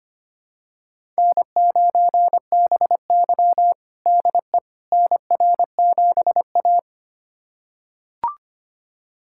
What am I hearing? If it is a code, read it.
Morse code, heard as a keyed tone, about 25 wpm, 700 Hz. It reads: N9BY DE NR7A
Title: USA Callsign DE Callsign - 4 Characters - 25wpm